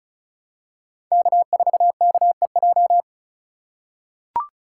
Morse code, heard as a keyed tone, about 36 wpm, 700 Hz. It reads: K4KEJ
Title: USA Call Signs - 5 Characters - 3x Character Spacing - 35wpm